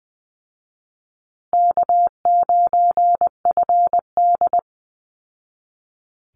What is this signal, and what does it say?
Morse code, heard as a keyed tone, about 20 wpm, 700 Hz. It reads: K9FD
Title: Easy Call Signs Repeated 2x - 20wpm